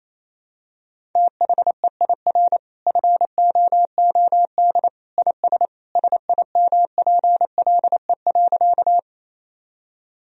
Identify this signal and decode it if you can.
Morse code, heard as a keyed tone, about 28 wpm, 700 Hz. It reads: THEIR FOOD IS SIMPLE.
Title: Sentences from Top 1000 Words - Review Part 1 - 28wpm